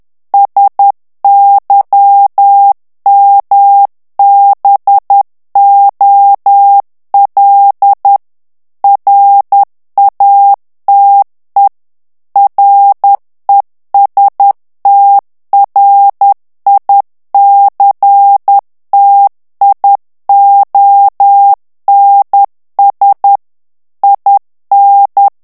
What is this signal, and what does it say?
Morse code, heard as a keyed tone, about 10 wpm, 800 Hz. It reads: SYMBOL RATE RESTRICTIONS IN
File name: ANGLAIS texte clair 3 vitesse 10 mots minute